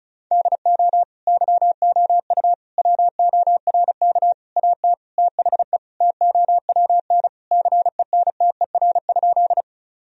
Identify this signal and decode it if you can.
Morse code, heard as a keyed tone, about 34 wpm, 700 Hz. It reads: DO YOU WORK AT THE TOWN CENTER?
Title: Sentences from Top 1000 Words - Review Part 2 - 35wpm